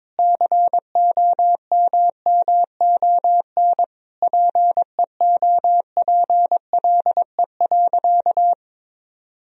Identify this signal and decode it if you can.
Morse code, heard as a keyed tone, about 22 wpm, 700 Hz. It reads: COMMON PEOPLE.
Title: Sentences from Top 1000 Words - Review Part 3 - 22wpm